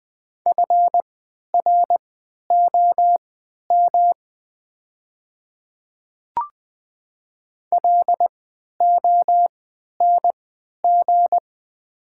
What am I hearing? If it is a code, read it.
Morse code, heard as a keyed tone, about 20 wpm, spaced as for 11 wpm, 700 Hz. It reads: FROM LONG
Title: 3x Character Spacing - Top 200 Words - 20wpm